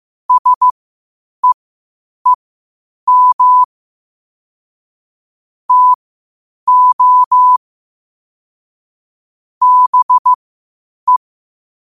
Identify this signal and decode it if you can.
Morse code, heard as a keyed tone, about 15 wpm, spaced as for 8 wpm, 1000 Hz. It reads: SEEM TO BE
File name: qotdfiles / 8wpm